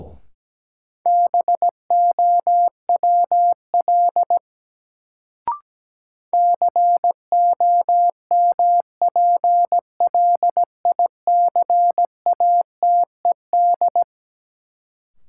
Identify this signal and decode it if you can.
Morse code, heard as a keyed tone, about 17 wpm, 700 Hz. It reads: BOWL COMPLICATED
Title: Sentences from Top 1200 Words - New Word Drill - 17wpm